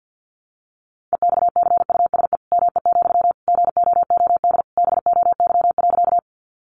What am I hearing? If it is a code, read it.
Morse code, heard as a keyed tone, about 50 wpm, 700 Hz. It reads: EXCUSE ME, GOOD BOY.
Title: Sentences from Top 700 Words 50wpm